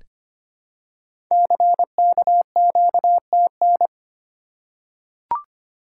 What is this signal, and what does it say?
Morse code, heard as a keyed tone, about 25 wpm, 690 Hz. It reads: CKQTN
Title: Code Groups - 5 Letters - 25wpm